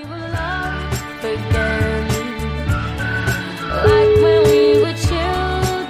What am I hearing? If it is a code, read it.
Ringback tone (Europe, 425 Hz), with noise.